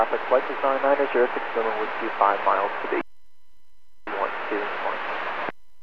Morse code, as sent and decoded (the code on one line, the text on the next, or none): none